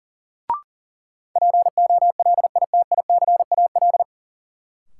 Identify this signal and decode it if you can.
Morse code, heard as a keyed tone, about 40 wpm, 700 Hz. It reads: POLITICAL
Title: Sentences from Top 800 Words - New Word Drill - 40wpm